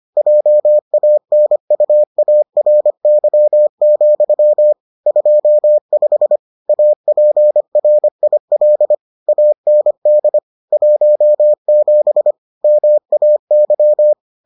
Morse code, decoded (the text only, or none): JANUARY, 25 APRIL AND 17 MAY